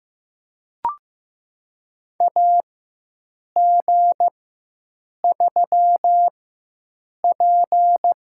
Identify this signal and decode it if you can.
Morse code, heard as a keyed tone, about 15 wpm, spaced as for 7 wpm, 700 Hz. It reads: AG3P
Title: USA Call Signs - 4 Characters - 4x Character Spacing - 15wpm